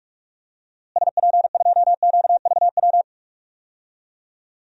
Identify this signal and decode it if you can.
Morse code, heard as a keyed tone, about 41 wpm, 700 Hz. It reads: SP2QVW
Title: DXCC-100 Call Signs - 6 Characters - Repeated 3x - 45wpm